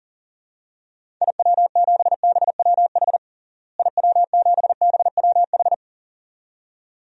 Morse code, heard as a keyed tone, about 40 wpm, 700 Hz. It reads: IW7BWH IW7BWH
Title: DXCC-100 Call Signs - 6 Characters - Repeated 2x - 40wpm